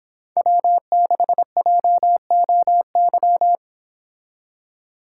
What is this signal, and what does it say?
Morse code, heard as a keyed tone, about 25 wpm, 700 Hz. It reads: W6JOY